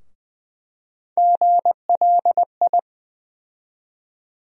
Morse code, heard as a keyed tone, about 20 wpm, 700 Hz. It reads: GLI
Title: USA Call Sign Postfixes - 3 Letters 20wpm